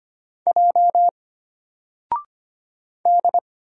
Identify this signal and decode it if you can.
Morse code, heard as a keyed tone, about 25 wpm, 700 Hz. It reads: J D
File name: Single-Letter-Number-25wpm